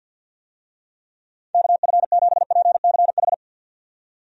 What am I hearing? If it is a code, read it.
Morse code, heard as a keyed tone, about 49 wpm, 700 Hz. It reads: KF7PCH
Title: USA Call Signs - 6 Characters - 50wpm